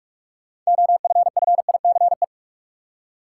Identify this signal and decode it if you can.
Morse code, heard as a keyed tone, about 46 wpm, 700 Hz. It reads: OFFICE
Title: Six Letter Words - 45wpm